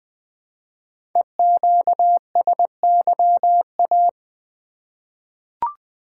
Morse code, heard as a keyed tone, about 20 wpm, 700 Hz. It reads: EQSYA
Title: Code Groups - 5 Letters - 20wpm